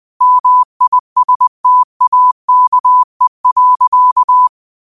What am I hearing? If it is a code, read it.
Morse code, heard as a keyed tone, about 20 wpm, 1000 Hz. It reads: MISTAKE.
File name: qotdfiles / 20wpm